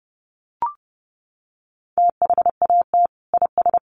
Morse code, heard as a keyed tone, about 30 wpm, 700 Hz. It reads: THAT IS
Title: Sentences from Top 900 Words 30wpm